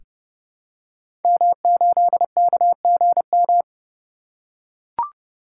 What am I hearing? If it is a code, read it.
Morse code, heard as a keyed tone, about 30 wpm, 700 Hz. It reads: M8KGM
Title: Code Groups - 5 Letter:Numbers - 30wpm